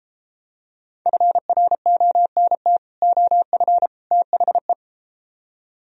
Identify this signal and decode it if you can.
Morse code, heard as a keyed tone, about 32 wpm, 700 Hz. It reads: FRONT OF THE